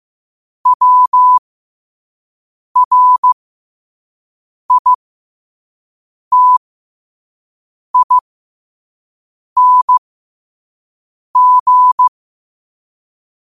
Morse code, heard as a keyed tone, about 15 wpm, spaced as for 5 wpm, 1000 Hz. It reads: WRITING
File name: qotdfiles / 5wpm